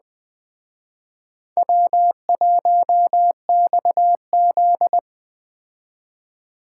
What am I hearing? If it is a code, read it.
Morse code, heard as a keyed tone, about 20 wpm, 700 Hz. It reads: W1XZ
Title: Speed Racing - USA 1x2 Call Signs 17wpm